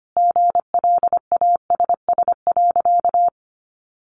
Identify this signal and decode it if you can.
Morse code, heard as a keyed tone, about 25 wpm, 700 Hz. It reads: GLASS.
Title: Sentences from Top 1400 Words 25wpm